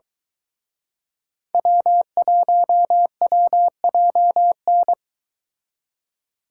Morse code, heard as a keyed tone, about 22 wpm, 700 Hz. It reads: W1WJN